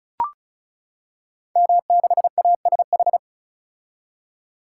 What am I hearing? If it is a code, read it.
Morse code, heard as a keyed tone, about 36 wpm, 700 Hz. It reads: M6ASH
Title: Code Groups - 5 Letter:Numbers - 35wpm